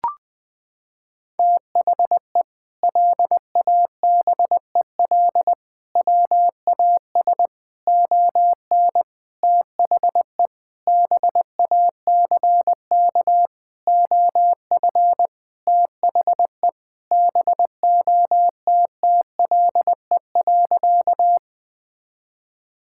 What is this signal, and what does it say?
Morse code, heard as a keyed tone, about 21 wpm, 700 Hz. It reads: THE LABEL WAS ON THE BACK OF THE BOTTLE.
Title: Sentences with 4-Word Phrases - 20wpm